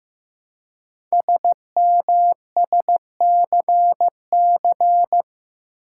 Morse code, heard as a keyed tone, about 15 wpm, 700 Hz. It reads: SMSCC
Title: Code Groups - 5 Letters - 15wpm